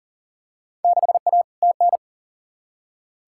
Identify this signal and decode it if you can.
Morse code, heard as a keyed tone, about 40 wpm, 700 Hz. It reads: 6A TN
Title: Field Day Exchanges Encore - 40wpm